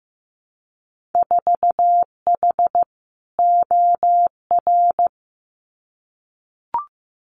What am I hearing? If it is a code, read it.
Morse code, heard as a keyed tone, about 15 wpm, 700 Hz. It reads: 4H OR
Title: Winter Field Day Exchanges Encore 15wpm